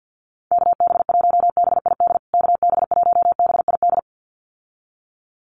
Morse code, heard as a keyed tone, about 50 wpm, 700 Hz. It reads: KB1BID KB1BID
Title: USA Call Signs Repeated 2x 50wpm